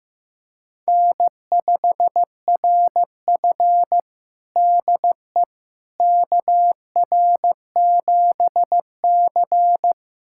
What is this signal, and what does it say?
Morse code, heard as a keyed tone, about 15 wpm, 700 Hz. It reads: N5RF DE KR7C
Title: USA Callsign DE Callsign - 4 Characters - 15wpm